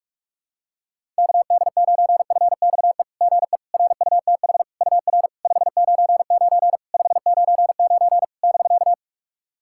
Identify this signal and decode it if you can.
Morse code, heard as a keyed tone, about 44 wpm, 700 Hz. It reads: KD9FXE GE RUTH UR 599 599 <BK>
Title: SOTA QSO Practice 45wpm